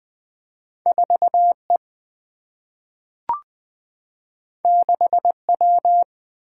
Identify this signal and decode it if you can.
Morse code, heard as a keyed tone, about 20 wpm, 700 Hz. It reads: 4 E 6 W
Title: Single-Letter Single-Number 20wpm